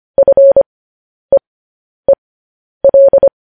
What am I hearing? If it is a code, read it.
Morse code, heard as a keyed tone, about 25 wpm, spaced as for 10 wpm, 550 Hz. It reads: FEEL
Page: News Headlines in Morse Code at 10 WPM